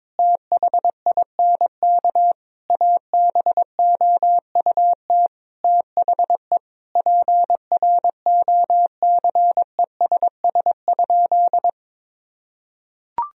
Morse code, heard as a keyed tone, about 22 wpm, 700 Hz. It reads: THINK ABOUT THE PROCESS?
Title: Sentences from Top 1000 Words - Review Part 3 - 22wpm